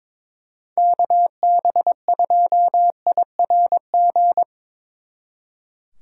Morse code, heard as a keyed tone, about 24 wpm, 700 Hz. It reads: KB2IRG
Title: Speed Racing - USA 2x3 Call Signs 22wpm